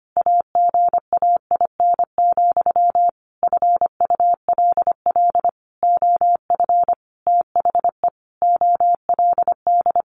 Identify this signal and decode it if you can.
Morse code, heard as a keyed tone, about 25 wpm, 700 Hz. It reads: AGAIN, FULL OF THE OLD
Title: A Sound Of Thunder 25wpm